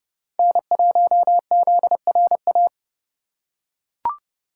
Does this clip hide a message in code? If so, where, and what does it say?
Yes: Morse code from 0.4 to 2.7 s, heard as a keyed tone, about 31 wpm, 700 Hz. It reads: N1ZRA